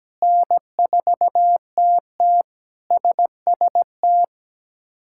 Morse code, heard as a keyed tone, about 17 wpm, 700 Hz. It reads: N4TT SST
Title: K1USN SST QSO Practice - 17wpm